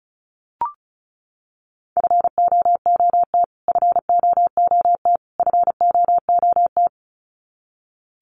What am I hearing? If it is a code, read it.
Morse code, heard as a keyed tone, about 34 wpm, 700 Hz. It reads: FOOT FOOT FOOT
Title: Top 500 Words Repeated 35wpm